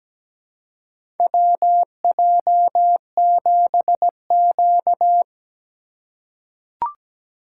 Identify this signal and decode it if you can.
Morse code, heard as a keyed tone, about 17 wpm, 700 Hz. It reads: WJ7Q
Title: USA Call Signs - 2x Character Spacing - 17wpm